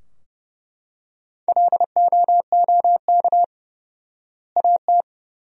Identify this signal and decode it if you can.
Morse code, heard as a keyed tone, about 30 wpm, 700 Hz. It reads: LOOK AT
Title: 4x Word-Spacing - Sentences from Top 100 Words - 30wpm